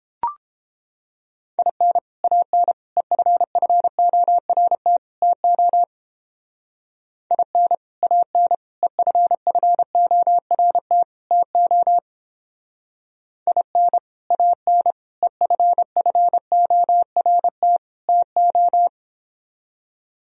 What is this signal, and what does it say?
Morse code, heard as a keyed tone, about 29 wpm, 700 Hz. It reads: IN AN EFFORT TO IN AN EFFORT TO IN AN EFFORT TO